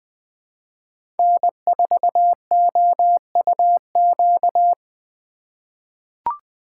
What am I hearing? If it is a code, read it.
Morse code, heard as a keyed tone, about 20 wpm, 700 Hz. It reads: N4OUQ
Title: USA Call Signs - 4x Character Spacing - 20wpm